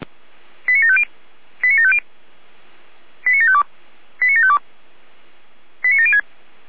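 Five-tone selective call (ZVEI), heard as quick runs of five tones, tones of about 70 ms. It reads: 89750 89750 89752 89752 89706
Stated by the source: ZVEI five-tone selective call received by SDR.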